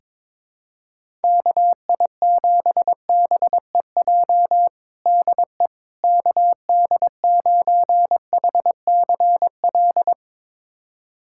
Morse code, heard as a keyed tone, about 22 wpm, 700 Hz. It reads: KI7BEJ DE KD9HCL
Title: USA Callsign DE Callsign - 6 Characters - 22wpm